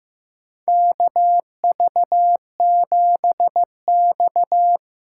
Morse code, heard as a keyed tone, about 15 wpm, 700 Hz. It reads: KV7X
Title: USA Call Signs - 4 Characters - 3x Character Spacing - 15wpm